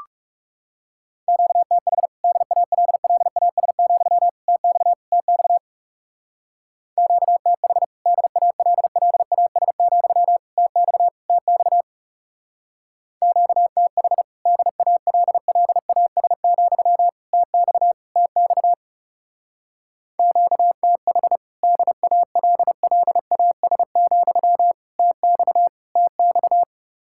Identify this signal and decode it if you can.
Morse code, heard as a keyed tone, about 35 wpm, 700 Hz. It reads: QTH DALLAS, TX TX QTH DALLAS, TX TX QTH DALLAS, TX TX QTH DALLAS, TX TX
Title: Speed-Racing Common QSO Phrases - 30wpm